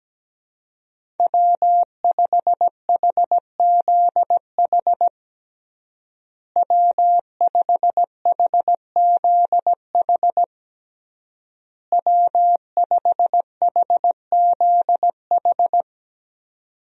Morse code, heard as a keyed tone, about 17 wpm, 700 Hz. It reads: W5HZH W5HZH W5HZH
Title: USA Call Signs - 5 Characters - Repeated 3x - 17wpm